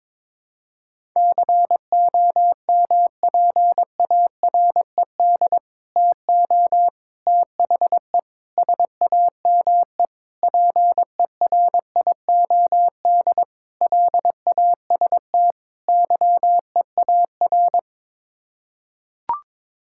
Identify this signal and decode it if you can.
Morse code, heard as a keyed tone, about 22 wpm, 700 Hz. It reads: COMPARED TO THE SAME PERIOD LAST YEAR
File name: practice / 22wpm / Sets of 7 Words - 22wpm